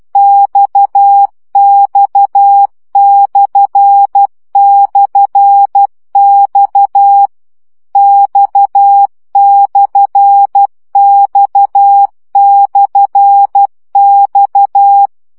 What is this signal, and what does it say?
Morse code, heard as a keyed tone, about 12 wpm, 800 Hz. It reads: XX//X X/X/X
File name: CW gyakorlas x per 10 baud